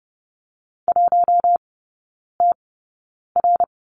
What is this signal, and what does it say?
Morse code, heard as a keyed tone, about 30 wpm, spaced as for 9 wpm, 700 Hz. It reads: 1TR
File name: Lesson 18e - Letter m - Three Characters 30wpm